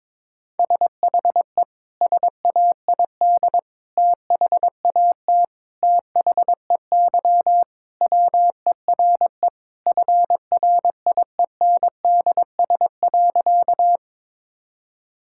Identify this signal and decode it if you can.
Morse code, heard as a keyed tone, about 22 wpm, 700 Hz. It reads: SHE SAID THAT THEY WERE FRIENDS.
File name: practice / 22wpm / Sentences with 3-Word Phrases - 22wpm